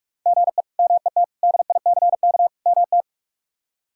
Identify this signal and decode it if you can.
Morse code, heard as a keyed tone, about 46 wpm, 700 Hz. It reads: GE MET DICK MT
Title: K1USN SST QSO Practice - 45wpm